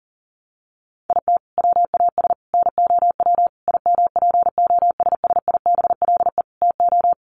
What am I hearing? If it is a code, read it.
Morse code, heard as a keyed tone, about 40 wpm, 700 Hz. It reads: IT WAS NOW IMPOSSIBLE TO
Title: Sentences from Top 1100 Words 40wpm